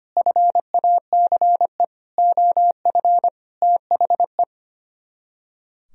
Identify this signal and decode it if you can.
Morse code, heard as a keyed tone, about 32 wpm, 700 Hz. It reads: FACE OF THE